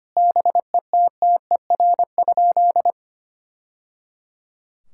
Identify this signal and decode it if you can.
Morse code, heard as a keyed tone, about 25 wpm, 700 Hz. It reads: BETTER?
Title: Sentences from Top 1000 Words - Review Part 6 - 25wpm